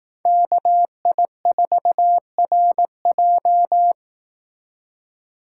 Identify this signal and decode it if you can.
Morse code, heard as a keyed tone, about 18 wpm, 700 Hz. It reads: KI4RJ
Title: Speed Racing - USA 2x2 Call Signs 15wpm